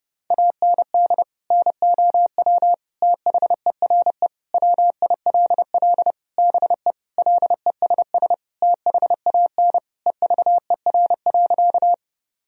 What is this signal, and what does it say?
Morse code, heard as a keyed tone, about 30 wpm, 700 Hz. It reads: AND NOW THERE WILL BE LESS THAN EVER.
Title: Sentences from Top 1000 Words - Review Part 2 - 30wpm